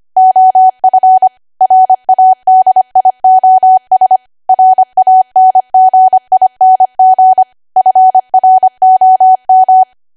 Morse code, heard as a keyed tone, about 25 wpm, 750 Hz. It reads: OF RADIOS RANGING FROM
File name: Archive / 25 WPM / 25WPM